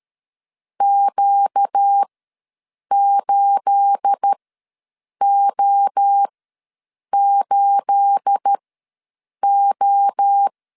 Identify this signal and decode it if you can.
Morse code, heard as a keyed tone, about 13 wpm, spaced as for 7 wpm, 800 Hz. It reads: Q8O8O
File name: LEZIONI / 9 28 ws 12wpm 3354